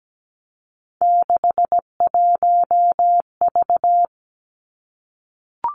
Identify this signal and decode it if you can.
Morse code, heard as a keyed tone, about 17 wpm, 700 Hz. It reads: 61V
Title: Three Challenging Letter-Number Combinations 17wpm